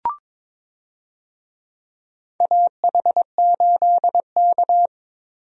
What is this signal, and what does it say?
Morse code, heard as a keyed tone, about 22 wpm, 700 Hz. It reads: AH8K
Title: USA Call Signs - 4 Characters - Repeated 3x - 22wpm